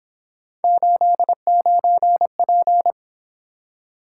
Morse code, heard as a keyed tone, about 28 wpm, 700 Hz. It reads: 89P